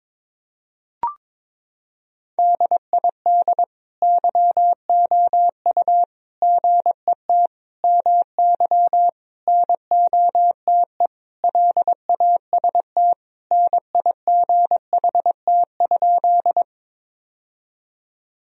Morse code, heard as a keyed tone, about 22 wpm, 700 Hz. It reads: DID YOU GET MY NOTE LAST NIGHT?
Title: Sentences from Top 1000 Words - Review Part 2 - 22wpm